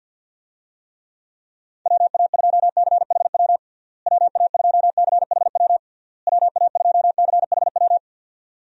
Morse code, heard as a keyed tone, about 48 wpm, 700 Hz. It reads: WA2CHW WA2CHW WA2CHW
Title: USA Call Signs - 6 Characters - Repeated 3x - 50wpm